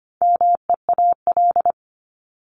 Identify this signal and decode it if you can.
Morse code, heard as a keyed tone, about 25 wpm, 700 Hz. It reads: MEAL
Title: Top 1751-2000 Words 25wpm